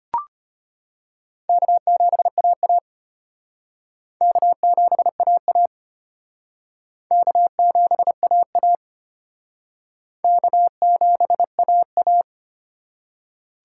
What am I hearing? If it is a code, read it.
Morse code, heard as a keyed tone, about 34 wpm, 700 Hz. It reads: K7AA K7AA K7AA K7AA